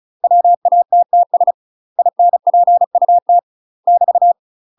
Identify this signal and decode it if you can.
Morse code, heard as a keyed tone, about 36 wpm, 700 Hz. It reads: WATTS INPUT <BT>
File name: UK MASTER QSOs TRAINING / 35WPM / UK MASTER QSOs 0002